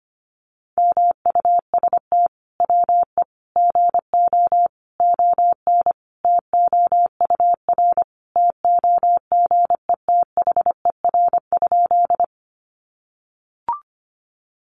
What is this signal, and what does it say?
Morse code, heard as a keyed tone, about 25 wpm, 700 Hz. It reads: MUST WE GO ON TOUR TOGETHER?
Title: Sentences from Top 1400 Words 25wpm